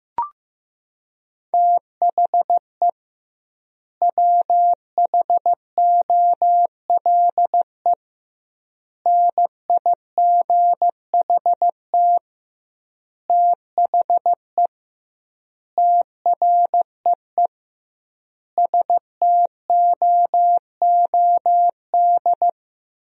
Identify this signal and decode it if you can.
Morse code, heard as a keyed tone, about 15 wpm, 700 Hz. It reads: THE WHOLE NIGHT THE TREE STOOD
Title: 2x Word-Spacing - Sentences from Top 500 Words - 15wpm